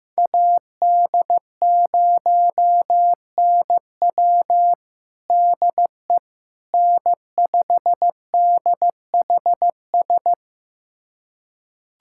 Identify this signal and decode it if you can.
Morse code, heard as a keyed tone, about 16 wpm, 700 Hz. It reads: AD0NW DE N5DHS
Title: USA Callsign DE Callsign - 5 Characters - 15wpm